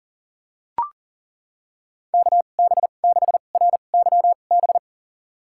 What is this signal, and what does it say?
Morse code, heard as a keyed tone, about 37 wpm, spaced as for 28 wpm, 700 Hz. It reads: KB6RYB
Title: USA Call Signs - 2x Character Spacing - 40wpm